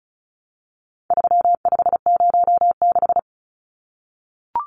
Morse code, heard as a keyed tone, about 34 wpm, 700 Hz. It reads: 3506
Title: Numbers - 4-Digit 35wpm